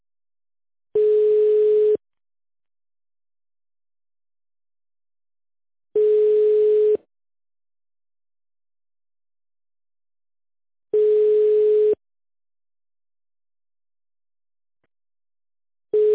Ringback tone (Europe, 425 Hz).